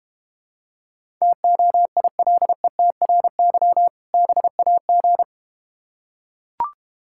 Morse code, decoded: TOILETRY BAG

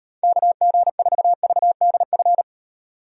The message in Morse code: KG4VDF

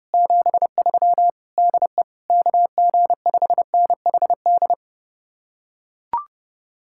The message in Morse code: 73 DE KG5NHD

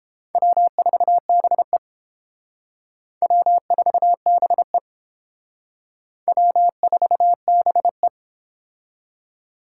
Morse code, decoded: W4BE W4BE W4BE

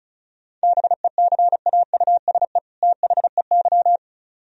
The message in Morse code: BECAUSE THEY

